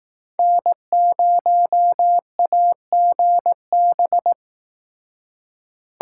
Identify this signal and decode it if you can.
Morse code, heard as a keyed tone, about 18 wpm, 700 Hz. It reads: N0AGB
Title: Speed Racing - USA 1x3 Call Signs 15wpm